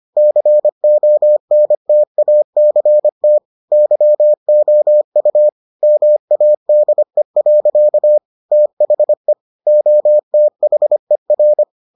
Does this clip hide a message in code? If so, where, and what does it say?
Yes: Morse code from 0.2 to 11.6 s, heard as a keyed tone, about 25 wpm, 600 Hz. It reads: CONTACT YOU MADE. THE OTHER